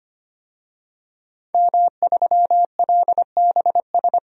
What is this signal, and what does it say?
Morse code, heard as a keyed tone, about 25 wpm, 700 Hz. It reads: M3LBS
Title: DXCC-100 Call Signs - 5 Characters - 25wpm